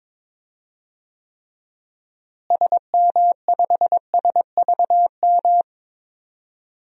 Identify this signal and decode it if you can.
Morse code, heard as a keyed tone, about 22 wpm, 700 Hz. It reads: SM5SVM
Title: DXCC-100 Call Signs - 6 Characters - Repeated 2x - 22wpm